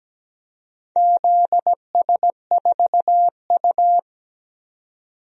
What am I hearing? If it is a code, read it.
Morse code, heard as a keyed tone, about 17 wpm, 700 Hz. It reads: ZS4U
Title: DXCC-100 Call Signs - 4 Characters - 17wpm